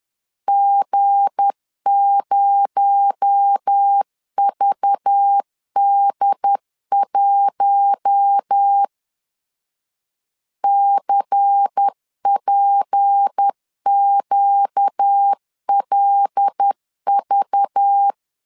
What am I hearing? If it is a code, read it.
Morse code, heard as a keyed tone, about 10 wpm, 800 Hz. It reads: G0VD1 CPQLV
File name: LEZIONI / 7 GF9Z / 7 G 10wpm 2501